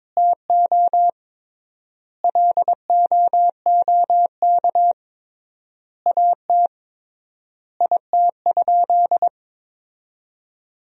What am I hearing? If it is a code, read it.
Morse code, heard as a keyed tone, about 22 wpm, 700 Hz. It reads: TO LOOK AT IT?
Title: 3x Word-Spacing - Sentences from Top 100 Words - 22wpm